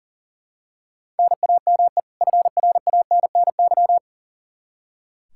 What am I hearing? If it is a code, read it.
Morse code, heard as a keyed tone, about 40 wpm, 700 Hz. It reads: NAME FRANNY
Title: Practice QSO - Exchange 1 - Beginning - 40wpm